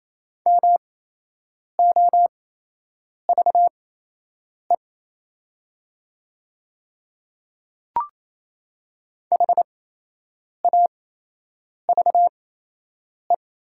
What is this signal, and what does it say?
Morse code, heard as a keyed tone, about 28 wpm, spaced as for 8 wpm, 700 Hz. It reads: MOVE HAVE
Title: 8x Character Spacing - Top 200 Words - 28wpm